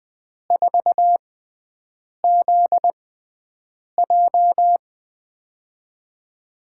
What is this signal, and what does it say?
Morse code, heard as a keyed tone, about 20 wpm, spaced as for 7 wpm, 700 Hz. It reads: 4ZJ
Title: USA Call Signs - 4 Characters - 6x Character Spacing - 20wpm